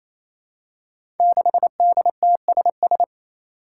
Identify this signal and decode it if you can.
Morse code, heard as a keyed tone, about 29 wpm, 700 Hz. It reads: 6DTSS
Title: Five Challenging Letter-Number Combinations 28wpm